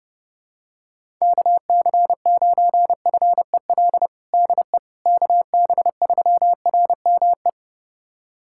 Morse code, heard as a keyed tone, about 30 wpm, 700 Hz. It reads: KC9FEL DE KB3RME
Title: USA Callsign DE Callsign - 6 Characters - 30wpm